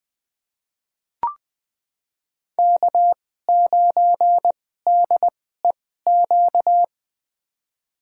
Morse code, heard as a keyed tone, about 20 wpm, spaced as for 15 wpm, 700 Hz. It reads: K9DEQ
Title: USA Call Signs - 5 Characters - 2x Character Spacing - 20wpm